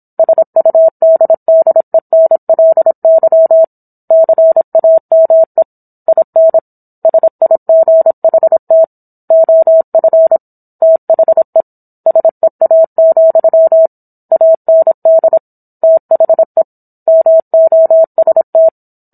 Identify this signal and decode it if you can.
Morse code, heard as a keyed tone, about 27 wpm, 650 Hz. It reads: SUDDENLY CAME IN SIGHT OF THE SEA, AND THE MOST